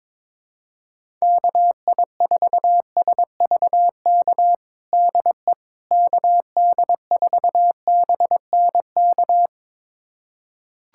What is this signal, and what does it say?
Morse code, heard as a keyed tone, about 22 wpm, 700 Hz. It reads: KI4SVK DE KD4BNK
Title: USA Callsign DE Callsign - 6 Characters - 22wpm